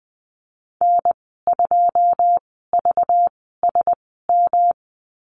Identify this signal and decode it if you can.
Morse code, heard as a keyed tone, about 20 wpm, spaced as for 14 wpm, 700 Hz. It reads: N2VSM
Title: 213 Lesson 34g - Number 8 - Call Signs 20wpm